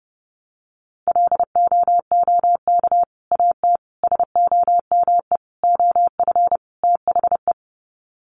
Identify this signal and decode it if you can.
Morse code, heard as a keyed tone, about 30 wpm, 700 Hz. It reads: LOOK AT SOME OF THE
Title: Sets of 5 Words Encore 30wpm